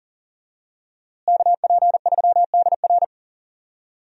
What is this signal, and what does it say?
Morse code, heard as a keyed tone, about 39 wpm, 700 Hz. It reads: KP3DR
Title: USA Call Signs - 5 Characters - Repeated 2x - 40wpm